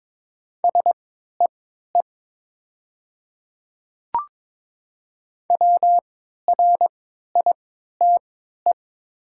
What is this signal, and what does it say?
Morse code, heard as a keyed tone, about 22 wpm, spaced as for 12 wpm, 700 Hz. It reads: SEE WRITE
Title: 3x Character Spacing - Top 100 Words - 22wpm